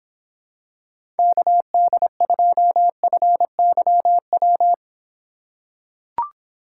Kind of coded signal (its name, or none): Morse code